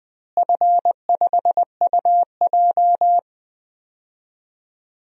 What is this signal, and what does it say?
Morse code, heard as a keyed tone, about 20 wpm, 700 Hz. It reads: F5UJ
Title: DXCC-100 Call Signs - 4 Characters - Repeated 2x - 20wpm